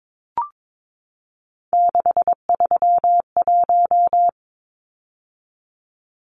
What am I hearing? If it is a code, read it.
Morse code, heard as a keyed tone, about 22 wpm, 700 Hz. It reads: 631
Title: Numbers - 3-Digit 22wpm